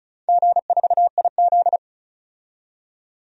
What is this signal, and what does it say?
Morse code, heard as a keyed tone, about 34 wpm, 700 Hz. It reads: G4IZ
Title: DXCC-100 Call Signs - 4 Characters - 35wpm